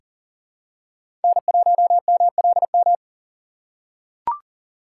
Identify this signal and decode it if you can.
Morse code, heard as a keyed tone, about 40 wpm, 700 Hz. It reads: N1MLM
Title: USA Call Signs - 5 Characters - 2x Character Spacing - 40wpm